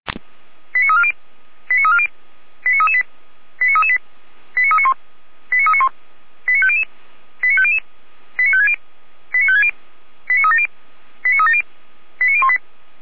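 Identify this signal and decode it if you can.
Five-tone selective call (ZVEI), heard as quick runs of five tones, tones of about 70 ms. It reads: 89340 89340 89338 89338 89391 89391 89500 89500 89560 89560 89370 89370 89018
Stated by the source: ZVEI five-tone call received by SDR, fire brigade Feuerwehr Orlamünde.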